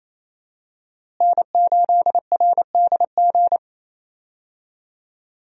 Morse code, heard as a keyed tone, about 28 wpm, 700 Hz. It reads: N8RDG
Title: USA Call Signs - 5 Characters - Repeated 2x - 28wpm